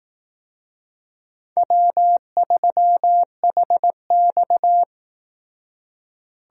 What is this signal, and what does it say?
Morse code, heard as a keyed tone, about 18 wpm, 700 Hz. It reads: W3HX
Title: Speed Racing - USA 1x2 Call Signs 15wpm